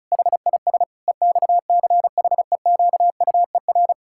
Morse code, heard as a keyed tone, about 36 wpm, 700 Hz. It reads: HIS EXCHEQUER
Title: Pangram Challenge v2 - 35wpm